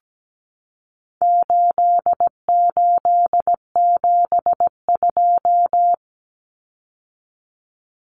Morse code, heard as a keyed tone, about 17 wpm, 700 Hz. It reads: 8872
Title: Numbers - 4-Digit 17wpm